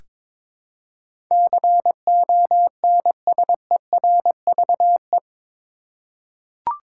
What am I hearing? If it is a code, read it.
Morse code, heard as a keyed tone, about 22 wpm, 700 Hz. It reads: CONSERVE
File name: practice / 22wpm / Eight Letter Words - 22wpm